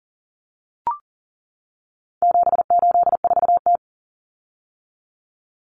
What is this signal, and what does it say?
Morse code, heard as a keyed tone, about 40 wpm, 700 Hz. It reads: 784T
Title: Common Cut Numbers - 4-Digit 40wpm